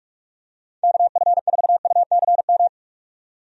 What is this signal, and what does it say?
Morse code, heard as a keyed tone, about 46 wpm, 700 Hz. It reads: KF4UCM
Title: USA Call Signs - 6 Characters - 45wpm